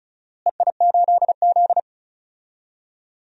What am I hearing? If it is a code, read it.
Morse code, heard as a keyed tone, about 34 wpm, 700 Hz. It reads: EI8Z
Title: DXCC-100 Call Signs - 4 Characters - Repeated 3x - 35wpm